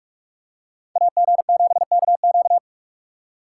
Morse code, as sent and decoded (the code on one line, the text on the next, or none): .- --. --... -.- --.-
AG7KQ